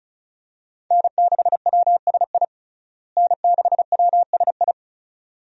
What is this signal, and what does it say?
Morse code, heard as a keyed tone, about 34 wpm, 700 Hz. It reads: N6WSI N6WSI
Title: USA Call Signs - 5 Characters - Repeated 2x - 35wpm